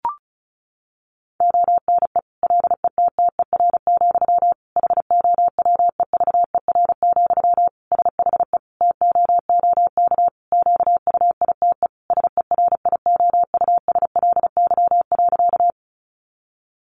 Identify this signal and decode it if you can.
Morse code, heard as a keyed tone, about 34 wpm, 700 Hz. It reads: ONE LETTER, HOWEVER, SHE TOOK QUITE SERIOUSLY.
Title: Sentences from Top 1200 Words 35wpm